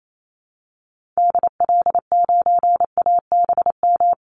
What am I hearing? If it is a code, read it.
Morse code, heard as a keyed tone, about 28 wpm, 700 Hz. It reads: DL9ABM
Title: Call Signs - 28wpm